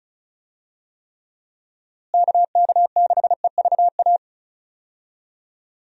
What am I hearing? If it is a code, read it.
Morse code, heard as a keyed tone, about 36 wpm, 700 Hz. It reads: KK6EVA
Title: USA Call Signs - 6 Characters - Repeated 2x - 35wpm